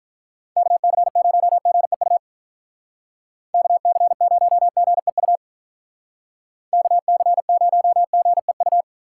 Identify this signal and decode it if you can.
Morse code, heard as a keyed tone, about 47 wpm, 700 Hz. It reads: KC0GEU KC0GEU KC0GEU